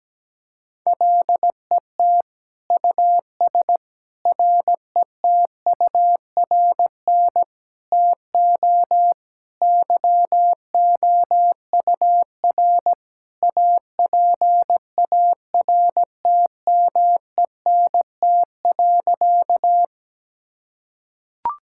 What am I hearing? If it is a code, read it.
Morse code, heard as a keyed tone, about 17 wpm, 700 Hz. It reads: LET US RETURN TO YOUR APARTMENT.
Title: Sentences from Top 1000 Words - Review Part 6 - 17wpm